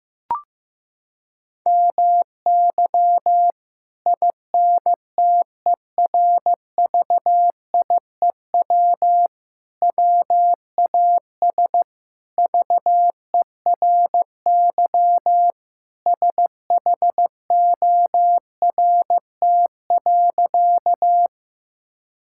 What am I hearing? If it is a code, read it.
Morse code, heard as a keyed tone, about 15 wpm, 700 Hz. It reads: MY INTERVIEW WAS VERY SHORT.
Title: Sentences from Top 1600 Words 15wpm